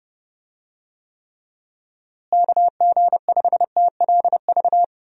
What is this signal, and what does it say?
Morse code, heard as a keyed tone, about 32 wpm, 700 Hz. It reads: KG5TLV
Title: USA Call Signs - 6 Characters - Repeated 3x - 30wpm